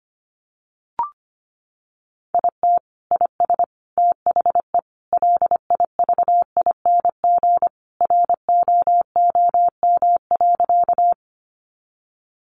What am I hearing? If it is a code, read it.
Morse code, heard as a keyed tone, about 25 wpm, 700 Hz. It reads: IT IS THE LIVING ROOM.
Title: Sentences from Top 1100 Words 25wpm